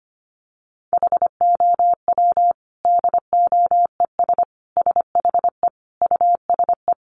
Morse code, heard as a keyed tone, about 25 wpm, 700 Hz. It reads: HOW DOES SHE USE
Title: Sentences from Top 200 Words 25wpm